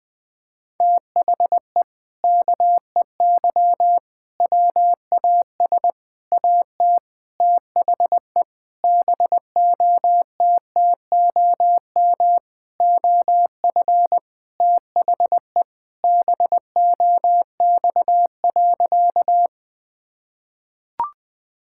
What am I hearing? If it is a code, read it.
Morse code, heard as a keyed tone, about 20 wpm, 700 Hz. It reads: THE KEY WAS AT THE BOTTOM OF THE BOX.
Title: Sentences with 4-Word Phrases - 20wpm